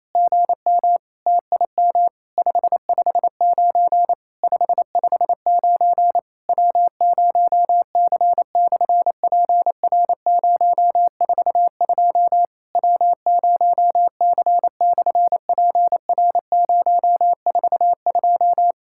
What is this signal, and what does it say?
Morse code, heard as a keyed tone, about 27 wpm, 700 Hz. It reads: GM TIM 559 559 W0C/PR042 W0C/PR042
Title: SOTA QSO Practice 28wpm